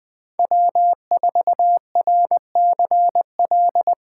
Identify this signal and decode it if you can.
Morse code, heard as a keyed tone, about 20 wpm, 700 Hz. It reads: W4RCL
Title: Speed Racing - USA 1x3 Call Signs 17wpm